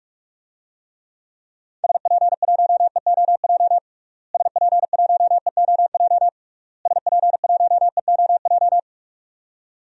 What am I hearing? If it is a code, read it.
Morse code, heard as a keyed tone, about 46 wpm, 700 Hz. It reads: SP1EOJ SP1EOJ SP1EOJ
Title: DXCC-100 Call Signs - 6 Characters - Repeated 3x - 45wpm